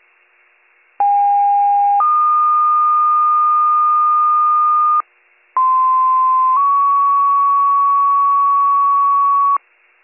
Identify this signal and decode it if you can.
Two-tone page, heard as a one-second tone, then a three-second tone, about 810 then 1220 Hz, about 1040 then 1120 Hz. It is heard from 1.0 to 9.6 s.